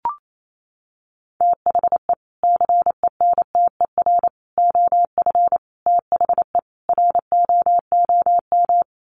Morse code, heard as a keyed tone, about 28 wpm, 700 Hz. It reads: THE CENTER OF THE ROOM
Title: Sets of 5 Words Encore 28wpm